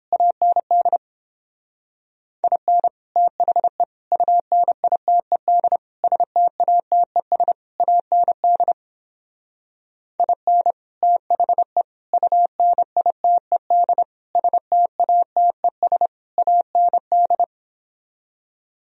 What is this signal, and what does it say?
Morse code, heard as a keyed tone, about 29 wpm, 700 Hz. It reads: AND IN THE UNITED STATES AND IN THE UNITED STATES AND